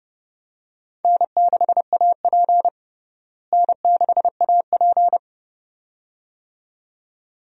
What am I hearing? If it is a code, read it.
Morse code, heard as a keyed tone, about 31 wpm, 700 Hz. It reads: N6AP N6AP
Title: USA Call Signs - 4 Characters - Repeated 2x - 30wpm